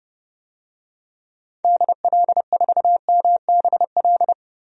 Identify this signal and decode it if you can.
Morse code, heard as a keyed tone, about 31 wpm, 700 Hz. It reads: DL4MBL
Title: DXCC-100 Call Signs - 6 Characters - 30wpm